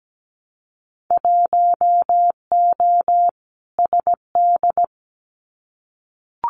Morse code, heard as a keyed tone, about 17 wpm, 700 Hz. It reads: 1O SD
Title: Winter Field Day Exchanges Encore 17wpm